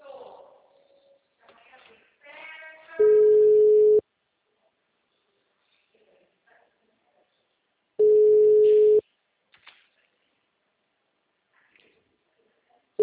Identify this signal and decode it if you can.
Ringback tone (Europe, 425 Hz), with speech.
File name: out- -2011- - - .332